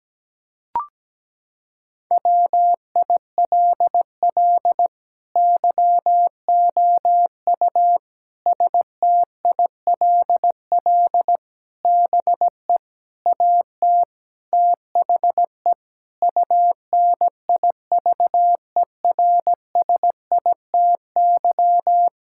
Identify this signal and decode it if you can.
Morse code, heard as a keyed tone, about 17 wpm, 700 Hz. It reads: WILL YOU STILL BE AT THE UNIVERSITY
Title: Sentences from Top 1000 Words - Review Part 5 - 17wpm